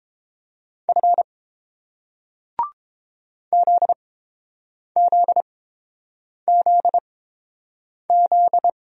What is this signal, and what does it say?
Morse code, heard as a keyed tone, about 29 wpm, spaced as for 7 wpm, 700 Hz. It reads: F ZZZZ